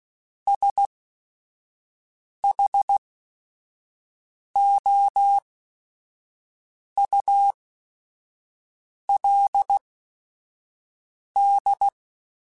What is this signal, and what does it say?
Morse code, heard as a keyed tone, about 16 wpm, spaced as for 5 wpm, 800 Hz. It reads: SHOULD
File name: Technical Subjects / Technical Subjects-mp3s / OR Gate @5